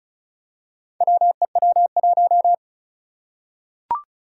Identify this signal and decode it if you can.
Morse code, heard as a keyed tone, about 36 wpm, 700 Hz. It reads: WEW1
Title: Four Challenging Letter-Number Combinations 35wpm